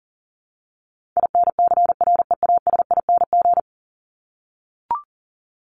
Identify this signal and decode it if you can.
Morse code, heard as a keyed tone, about 40 wpm, 700 Hz. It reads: INCREASING
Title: ING Suffix - Top 100 Words with Base Words 40wpm